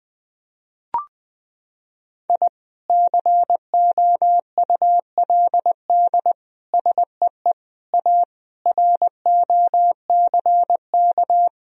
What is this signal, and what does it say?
Morse code, heard as a keyed tone, about 21 wpm, 700 Hz. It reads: I COULD SEE A ROCK
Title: Sentences from Top 1000 Words - Review Part 2 - 20wpm